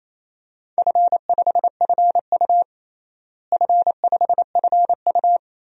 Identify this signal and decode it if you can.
Morse code, heard as a keyed tone, about 28 wpm, 700 Hz. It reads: F5FU F5FU
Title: DXCC-100 Call Signs - 4 Characters - Repeated 2x - 28wpm